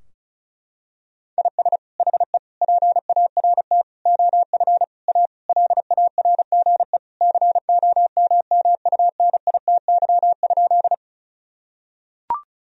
Morse code, heard as a keyed tone, about 36 wpm, 700 Hz. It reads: IS HE PART OF A LARGE COMMUNITY?
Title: Sentences from Top 1000 Words - Review Part 4 - 35wpm